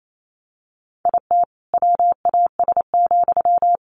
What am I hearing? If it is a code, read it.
Morse code, heard as a keyed tone, about 28 wpm, 700 Hz. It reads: IT WAS,
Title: Sentences from Top 1500 Words 28wpm